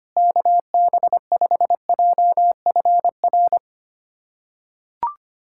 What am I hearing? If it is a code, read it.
Morse code, heard as a keyed tone, about 25 wpm, 700 Hz. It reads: KB5JFR
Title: USA Callsign DE Callsign - 6 Characters - 25wpm